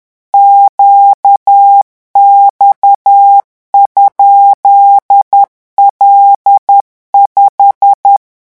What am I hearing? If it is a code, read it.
Morse code, heard as a keyed tone, about 10 wpm, 800 Hz. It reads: QX?L5
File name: morse mp3 / revision16-vitesse 10 mots